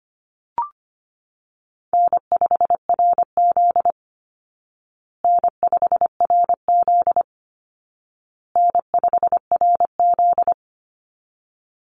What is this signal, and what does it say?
Morse code, heard as a keyed tone, about 25 wpm, 700 Hz. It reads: N5RZ N5RZ N5RZ
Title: Easy Call Signs Repeated 3x - 25wpm